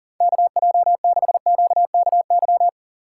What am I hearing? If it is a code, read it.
Morse code, heard as a keyed tone, about 37 wpm, 700 Hz. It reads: KJ6QKY
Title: USA Call Signs - 6 Characters - 3x Character Spacing - 40wpm